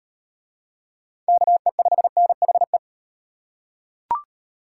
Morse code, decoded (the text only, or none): KE5NHE